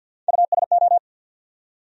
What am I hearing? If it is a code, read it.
Morse code, heard as a keyed tone, about 54 wpm, 700 Hz. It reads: USO
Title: USA Call Sign Postfixes - 3 Letters 50wpm